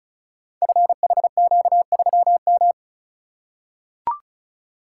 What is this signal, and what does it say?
Morse code, heard as a keyed tone, about 36 wpm, 690 Hz. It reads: FHQ3M
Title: Code Groups - 5 Letter:Numbers - 35wpm